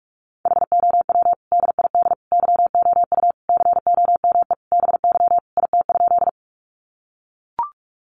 Morse code, heard as a keyed tone, about 44 wpm, 700 Hz. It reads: HOW DID YOU COME BY IT?
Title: Sentences from Top 100 Words 45wpm